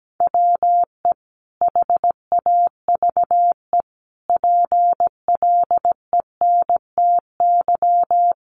Morse code, heard as a keyed tone, about 17 wpm, 700 Hz. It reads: WE HAVE PLENTY
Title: Sentences from Top 1300 Words 17wpm